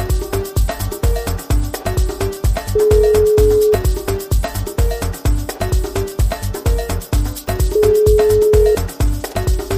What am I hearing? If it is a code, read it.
Ringback tone (Europe, 425 Hz).